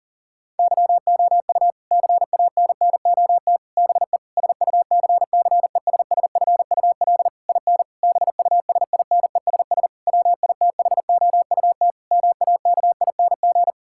Morse code, heard as a keyed tone, about 40 wpm, 700 Hz. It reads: YOU CANNOT BE SUCCESSFUL IN BUSINESS WITHOUT MAKING
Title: Sentences from Top 1000 Words - Review Part 6 - 40wpm